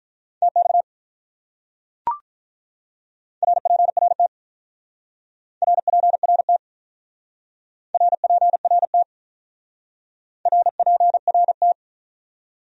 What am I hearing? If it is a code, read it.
Morse code, heard as a keyed tone, about 45 wpm, 700 Hz. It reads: TX RPRT RPRT RPRT RPRT